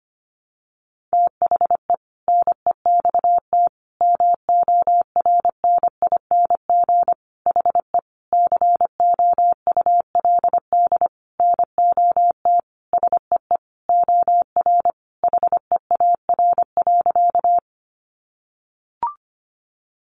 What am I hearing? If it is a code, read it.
Morse code, heard as a keyed tone, about 25 wpm, 700 Hz. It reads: THE NEXT MORNING HE COULD NOT SEE OR HEAR.
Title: Sentences from Top 600 Words 25wpm